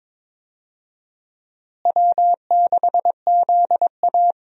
Morse code, heard as a keyed tone, about 22 wpm, 700 Hz. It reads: W6ZA
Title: USA Call Signs - 4 Characters - Repeated 2x - 22wpm